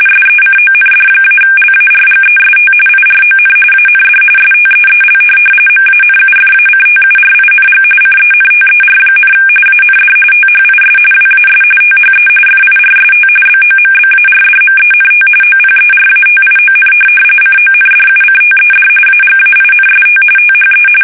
RTTY, about 45.45 baud. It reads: SECOND-LARGEST IRON AND STEEL MANUFACTURING CENTER, AND THIS WAS THE FIRST BOMBARDMENT OPERATION CONDUCTED AGAINST THE
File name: fsk850 / USS-Massachusetts